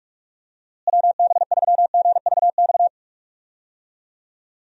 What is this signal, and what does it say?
Morse code, heard as a keyed tone, about 46 wpm, 700 Hz. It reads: WB3GVX
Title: USA Call Signs - 6 Characters - Repeated 2x - 45wpm